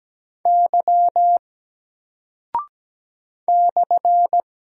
Morse code, heard as a keyed tone, about 17 wpm, 700 Hz. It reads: Y /
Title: Single Character (Letter:Number:Punctuation:Prosign) 17wpm